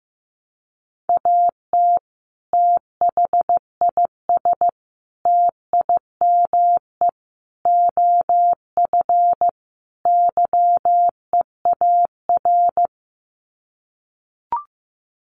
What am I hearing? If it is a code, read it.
Morse code, heard as a keyed tone, about 15 wpm, 700 Hz. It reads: AT THIS TIME OF YEAR
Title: Sets of 5 Words Encore2 15wpm